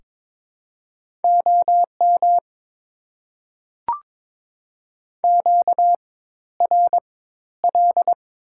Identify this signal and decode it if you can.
Morse code, heard as a keyed tone, about 22 wpm, spaced as for 12 wpm, 690 Hz. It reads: OM QRL
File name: Top 90 QSO Elements - 4x Character Spacing - 22wpm